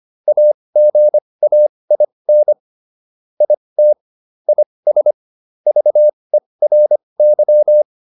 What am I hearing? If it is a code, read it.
Morse code, heard as a keyed tone, about 25 wpm, spaced as for 20 wpm, 600 Hz. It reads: AGAIN IT IS VERY
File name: SS209 20